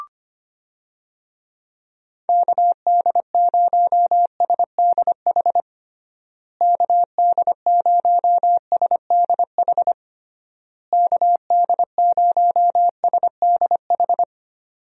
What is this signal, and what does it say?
Morse code, heard as a keyed tone, about 25 wpm, 700 Hz. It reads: KD0SDH KD0SDH KD0SDH
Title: USA Call Signs - 6 Characters - Repeated 3x - 25wpm